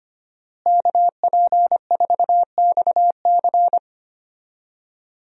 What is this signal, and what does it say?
Morse code, heard as a keyed tone, about 25 wpm, 700 Hz. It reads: KP4XC
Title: DXCC-100 Call Signs - 5 Characters - Repeated 3x - 25wpm